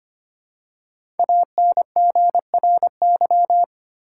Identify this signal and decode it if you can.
Morse code, heard as a keyed tone, about 25 wpm, 700 Hz. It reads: ANGRY
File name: 5 Letter Words - First 3 Letters Given - 25wpm